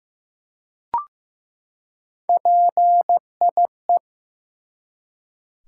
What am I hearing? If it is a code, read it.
Morse code, heard as a keyed tone, about 15 wpm, 700 Hz. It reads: PIE
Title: Three Letter Words - 15wpm